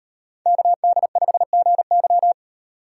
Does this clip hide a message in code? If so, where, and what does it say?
Yes: Morse code from 0.5 to 2.3 s, heard as a keyed tone, about 38 wpm, 700 Hz. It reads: KD5GY